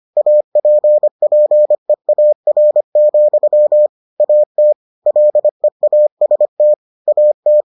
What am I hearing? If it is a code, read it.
Morse code, heard as a keyed tone, about 25 wpm, 600 Hz. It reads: APPEAR, AT LEAST AT